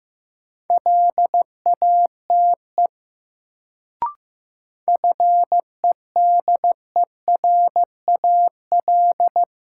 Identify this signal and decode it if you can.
Morse code, heard as a keyed tone, about 15 wpm, 700 Hz. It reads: LATE FEDERAL
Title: Sets of 3 Words - New Word Drill - 15wpm